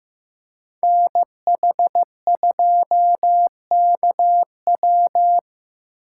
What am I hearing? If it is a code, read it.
Morse code, heard as a keyed tone, about 15 wpm, 700 Hz. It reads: NH2KW
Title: USA Call Signs - 2x Character Spacing - 15wpm